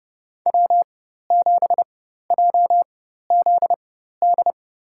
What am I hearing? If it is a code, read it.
Morse code, heard as a keyed tone, about 30 wpm, spaced as for 14 wpm, 700 Hz. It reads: W7JZD
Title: USA Call Signs - 5 Characters - 4x Character Spacing - 30wpm